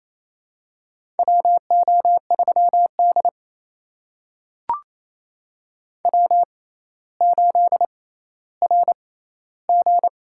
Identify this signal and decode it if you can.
Morse code, heard as a keyed tone, about 28 wpm, spaced as for 14 wpm, 700 Hz. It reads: WO3D W8RG
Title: USA Call Signs - 4 Characters - 6x Character Spacing - 28wpm